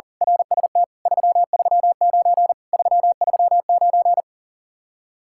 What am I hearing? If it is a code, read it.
Morse code, heard as a keyed tone, about 38 wpm, 700 Hz. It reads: RST 339 339
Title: Speed-Racing Common QSO Phrases - 40wpm